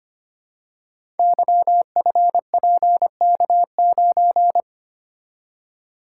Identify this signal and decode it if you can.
Morse code, heard as a keyed tone, about 25 wpm, 700 Hz. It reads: YFPK9
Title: Code Groups - 5 Letter:Numbers - 25wpm